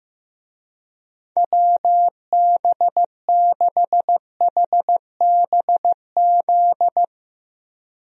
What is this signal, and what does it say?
Morse code, heard as a keyed tone, about 15 wpm, 700 Hz. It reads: WB6HBZ
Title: USA Call Signs - 6 Characters - 15wpm